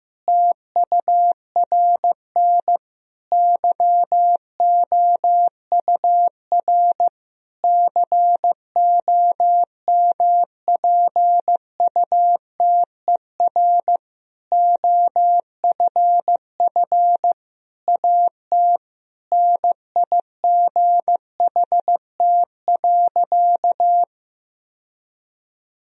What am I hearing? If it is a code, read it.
Morse code, heard as a keyed tone, about 15 wpm, 700 Hz. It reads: TURN YOUR COMPUTER OFF AT NIGHT.
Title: Sentences from Top 1000 Words - Review Part 3 - 15wpm